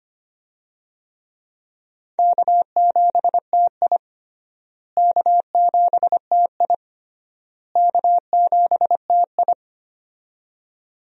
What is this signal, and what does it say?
Morse code, heard as a keyed tone, about 25 wpm, 700 Hz. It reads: K7TI K7TI K7TI
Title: USA Call Signs - 4 Characters - Repeated 3x - 25wpm